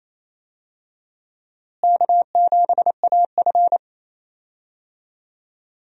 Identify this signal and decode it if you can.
Morse code, heard as a keyed tone, about 28 wpm, 700 Hz. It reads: K7AF
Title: USA Call Signs - 4 Characters - Repeated 3x - 28wpm